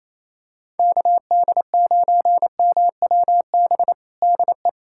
Morse code, heard as a keyed tone, about 28 wpm, 700 Hz. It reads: KD9MWB DE
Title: USA Callsign DE Callsign - 6 Characters - 28wpm